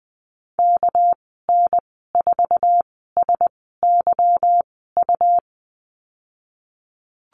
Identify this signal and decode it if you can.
Morse code, heard as a keyed tone, about 19 wpm, spaced as for 14 wpm, 700 Hz. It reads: KN4SYU
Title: 213 Lesson 34g - Number 8 - Call Signs 20wpm